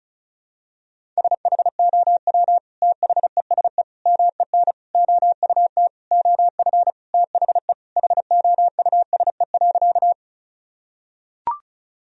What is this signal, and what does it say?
Morse code, heard as a keyed tone, about 36 wpm, 700 Hz. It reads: SHOW THESE MEN OUT OF THE HOUSE.
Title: Sentences from Top 1000 Words - Review Part 1 - 35wpm